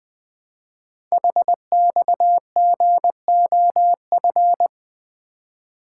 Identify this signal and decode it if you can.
Morse code, heard as a keyed tone, about 20 wpm, 700 Hz. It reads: HXGOF
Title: Code Groups - 5 Letters - 20wpm